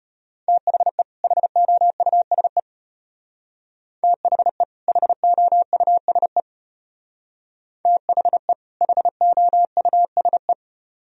Morse code, heard as a keyed tone, about 31 wpm, 700 Hz. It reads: THE HOUSE THE HOUSE THE HOUSE
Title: Speed-Racing - Sets of 2 Words 25wpm